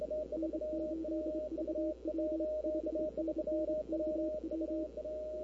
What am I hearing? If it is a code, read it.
Morse code, heard as a keyed tone, about 28 wpm, 310 Hz. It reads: N8VW 5NN KW
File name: CW / 2005-k5tr arrl cw dx -23:00